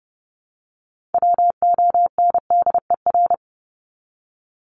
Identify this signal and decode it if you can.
Morse code, heard as a keyed tone, about 30 wpm, 700 Hz. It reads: WONDER
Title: Top 500 Words Repeated 30wpm